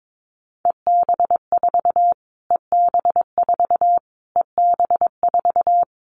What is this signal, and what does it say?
Morse code, heard as a keyed tone, about 22 wpm, 700 Hz. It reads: EB4 EB4 EB4
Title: Callsign Prefixes for Top 25 DXCC - Repeated 3x - 22wpm